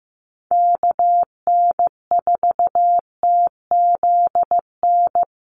Morse code, heard as a keyed tone, about 15 wpm, 700 Hz. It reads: KN4TZN
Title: USA Call Signs Repeated 2x 15wpm